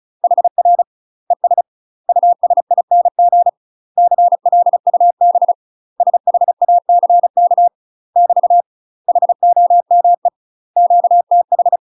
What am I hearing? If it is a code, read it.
Morse code, heard as a keyed tone, about 34 wpm, 700 Hz. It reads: HR ES USING CLUB SHACK <BT> HOME QTH
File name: UK MASTER QSOs TRAINING / 35WPM / UK MASTER QSOs 0024